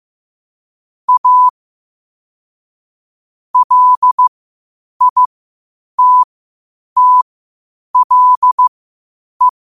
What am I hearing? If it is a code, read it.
Morse code, heard as a keyed tone, about 15 wpm, spaced as for 8 wpm, 1000 Hz. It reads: A LITTLE
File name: qotdfiles / 8wpm